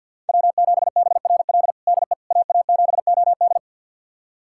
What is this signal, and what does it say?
Morse code, heard as a keyed tone, about 46 wpm, 700 Hz. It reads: W7BRL DE AA7OD
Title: USA Callsign DE Callsign - 5 Characters - 50wpm